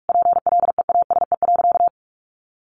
Morse code, heard as a keyed tone, about 44 wpm, 700 Hz. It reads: PLEASE.
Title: Sentences from Top 1500 Words 45wpm